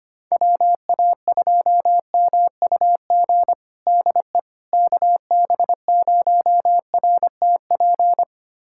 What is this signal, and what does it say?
Morse code, heard as a keyed tone, about 25 wpm, 700 Hz. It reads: WA2MUG DE KB0RTP
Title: USA Callsign DE Callsign - 6 Characters - 25wpm